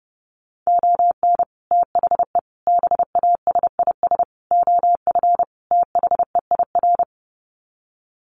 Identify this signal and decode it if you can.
Morse code, heard as a keyed tone, about 30 wpm, 700 Hz. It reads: ON THE BASIS OF THEIR
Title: Sets of 5 Words Encore 30wpm